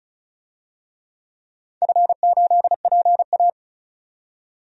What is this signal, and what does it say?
Morse code, heard as a keyed tone, about 36 wpm, 700 Hz. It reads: F8PA
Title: DXCC-100 Call Signs - 4 Characters - 35wpm